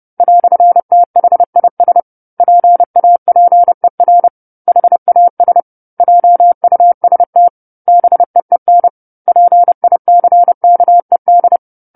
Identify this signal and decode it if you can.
Morse code, heard as a keyed tone, about 30 wpm, 700 Hz. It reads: "THIS PAPER HAS JUST BEEN PICKED
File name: 10-Chapter-10 30 WPM